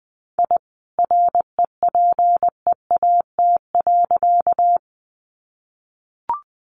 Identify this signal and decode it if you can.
Morse code, heard as a keyed tone, about 19 wpm, 700 Hz. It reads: I REPEAT.
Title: A Sound Of Thunder 20wpm